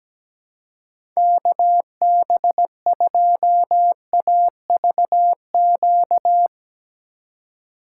Morse code, heard as a keyed tone, about 17 wpm, 700 Hz. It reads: KB2AVQ
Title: USA Call Signs - 6 Characters - 2x Character Spacing - 17wpm